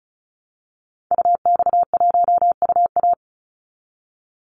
Morse code, heard as a keyed tone, about 34 wpm, 700 Hz. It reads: UX1UA
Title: Call Signs Repeated 3x - 35wpm